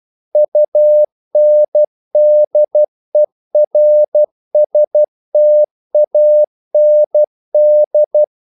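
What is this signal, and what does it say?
Morse code, heard as a keyed tone, about 12 wpm, 600 Hz. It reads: UNDERSTAND
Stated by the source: Morse practice sending English quotations.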